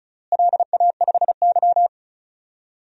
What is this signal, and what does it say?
Morse code, heard as a keyed tone, about 34 wpm, 700 Hz. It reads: LA5Y
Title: DXCC-100 Call Signs - 4 Characters - Repeated 3x - 35wpm